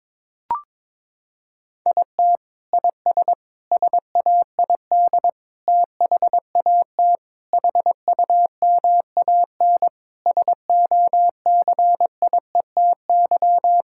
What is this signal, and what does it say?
Morse code, heard as a keyed tone, about 22 wpm, 700 Hz. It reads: IT IS SAID THAT HUMAN SOCIETY
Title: Sentences from Top 1000 Words - Review Part 6 - 22wpm